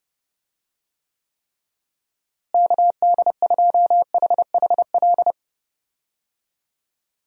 Morse code, heard as a keyed tone, about 32 wpm, 700 Hz. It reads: KD2HHL
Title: USA Call Signs - 6 Characters - Repeated 3x - 30wpm